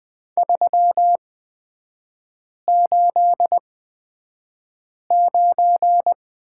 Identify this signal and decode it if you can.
Morse code, heard as a keyed tone, about 20 wpm, spaced as for 5 wpm, 700 Hz. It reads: 389
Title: 236 Lesson 38e - Letter q - Three Characters 20wpm